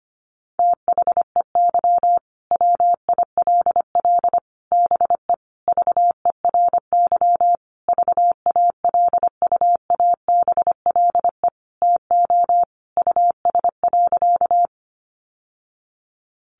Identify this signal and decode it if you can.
Morse code, heard as a keyed tone, about 25 wpm, 700 Hz. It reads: THEY WILL BE VERY VALUABLE TO US.
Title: Sentences from Top 1100 Words 25wpm